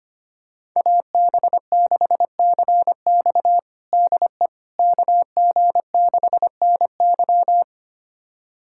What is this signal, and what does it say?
Morse code, heard as a keyed tone, about 25 wpm, 700 Hz. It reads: AB6CX DE KG6NY
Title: USA Callsign DE Callsign - 5 Characters - 25wpm